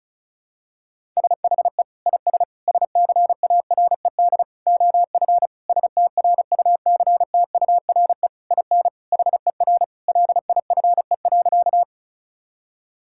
Morse code, heard as a keyed tone, about 34 wpm, 700 Hz. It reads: SHE IS SCARED OF STRUCTURE IN HER LIFE.
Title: Sentences from Top 1000 Words - Review Part 6 - 35wpm